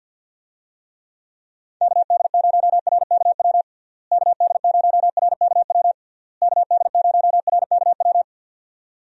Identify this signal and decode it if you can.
Morse code, heard as a keyed tone, about 52 wpm, 700 Hz. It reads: KD0RKW KD0RKW KD0RKW
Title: USA Call Signs - 6 Characters - Repeated 3x - 50wpm